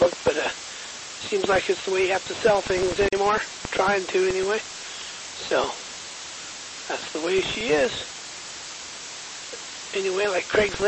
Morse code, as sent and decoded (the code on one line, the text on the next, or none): none